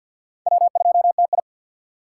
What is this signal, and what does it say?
Morse code, heard as a keyed tone, about 52 wpm, 700 Hz. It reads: W2TI
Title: K1USN SST QSO Practice - 50wpm